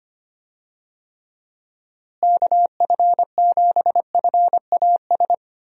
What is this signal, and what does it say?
Morse code, heard as a keyed tone, about 25 wpm, 700 Hz. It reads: KF7FAS
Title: USA Call Signs - 6 Characters - Repeated 2x - 25wpm